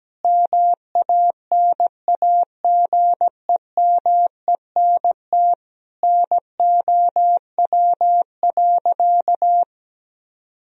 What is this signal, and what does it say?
Morse code, heard as a keyed tone, about 17 wpm, 700 Hz. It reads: MANAGEMENT NOW.
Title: Sentences from Top 1000 Words - Review Part 4 - 17wpm